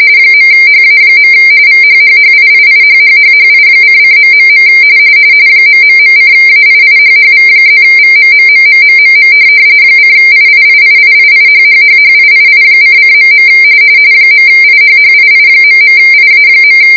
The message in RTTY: .HA. VHHHIIIHHHHI::... ...:AIIHHV+VMMMMMMM :IIIIIIIHHHHIII::.. .IHHII..IA.